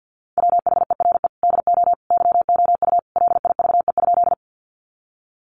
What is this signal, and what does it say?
Morse code, heard as a keyed tone, about 48 wpm, 700 Hz. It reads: WHERE DO YOU LIVE?
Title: Sentences from Top 200 Words 50wpm